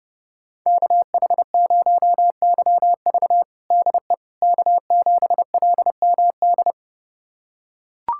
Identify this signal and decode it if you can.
Morse code, heard as a keyed tone, about 31 wpm, 700 Hz. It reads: KH0YV DE K7LMD
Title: USA Callsign DE Callsign - 5 Characters - 30wpm